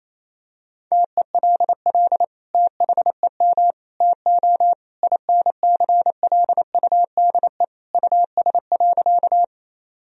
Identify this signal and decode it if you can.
Morse code, heard as a keyed tone, about 28 wpm, 700 Hz. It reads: TELL THEM TO INCLUDE US.
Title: Sentences from Top 1000 Words - Review Part 3 - 28wpm